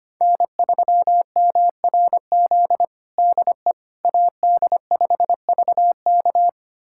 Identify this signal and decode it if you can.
Morse code, heard as a keyed tone, about 25 wpm, 700 Hz. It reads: N3MRZ DE AD5VK
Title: USA Callsign DE Callsign - 5 Characters - 25wpm